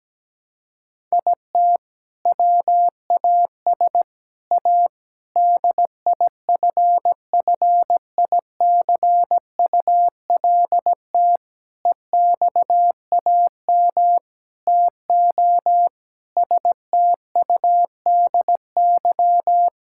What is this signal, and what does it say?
Morse code, heard as a keyed tone, about 17 wpm, 700 Hz. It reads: IT WAS A DIFFICULT EXAM TO STUDY
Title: Sentences from Top 1000 Words - Review Part 5 - 17wpm